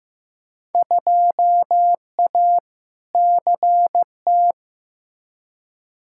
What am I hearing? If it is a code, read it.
Morse code, heard as a keyed tone, about 15 wpm, 700 Hz. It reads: 2A CT
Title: Field Day Exchanges Encore - 15wpm